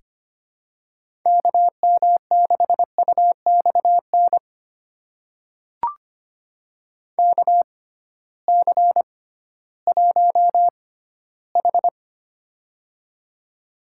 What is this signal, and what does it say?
Morse code, heard as a keyed tone, about 25 wpm, spaced as for 15 wpm, 700 Hz. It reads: KM6UXN KC1H
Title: USA Call Signs - 6x Character Spacing - 25wpm